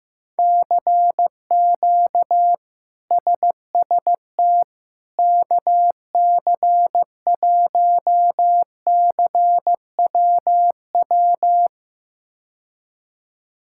Morse code, heard as a keyed tone, about 15 wpm, 700 Hz. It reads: CQ SST KC1CWW
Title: K1USN SST QSO Practice - 15wpm